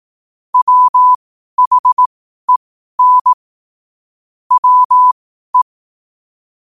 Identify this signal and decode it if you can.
Morse code, heard as a keyed tone, about 18 wpm, spaced as for 12 wpm, 1000 Hz. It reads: WHEN WE
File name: qotdfiles / 12wpm